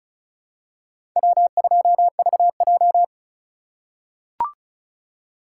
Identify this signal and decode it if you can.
Morse code, heard as a keyed tone, about 34 wpm, 700 Hz. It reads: W2VJ
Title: USA Call Signs - 4 Characters - 8x Character Spacing - 35wpm